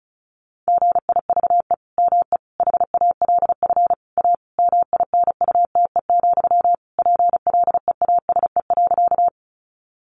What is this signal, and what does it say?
Morse code, heard as a keyed tone, about 34 wpm, 700 Hz. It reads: GIVE ME HALF A MINUTE, PLEASE.
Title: Sentences from Top 1500 Words 35wpm